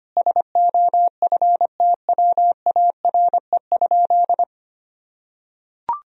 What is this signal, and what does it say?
Morse code, heard as a keyed tone, about 25 wpm, 700 Hz. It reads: SOFTWARE?
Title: Sentences from Top 1000 Words - Review Part 3 - 25wpm